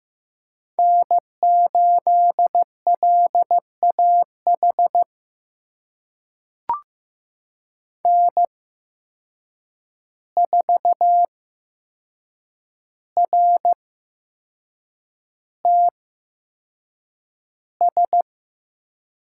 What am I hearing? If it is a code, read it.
Morse code, heard as a keyed tone, about 15 wpm, spaced as for 6 wpm, 700 Hz. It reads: N8LAH N4RTS
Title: USA Call Signs - 5 Characters - 8x Character Spacing - 15wpm